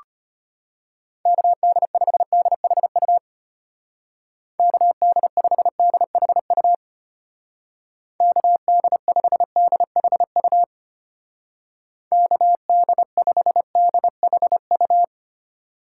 Morse code, heard as a keyed tone, about 30 wpm, 700 Hz. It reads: KD5DHU KD5DHU KD5DHU KD5DHU
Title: Speed Racing - USA Call Signs 25wpm